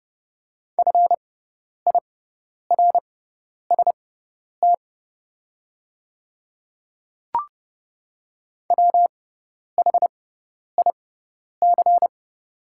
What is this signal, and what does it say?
Morse code, heard as a keyed tone, about 31 wpm, spaced as for 10 wpm, 700 Hz. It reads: FIRST WHIC
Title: 6x Character Spacing - Top 100 Words - 30wpm